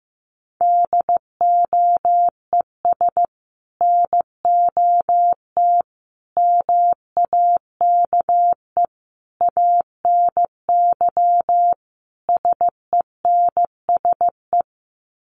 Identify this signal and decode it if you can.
Morse code, heard as a keyed tone, about 15 wpm, 700 Hz. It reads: DOES NOT MAKE ANY SENSE
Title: Sets of 5 Words Encore2 15wpm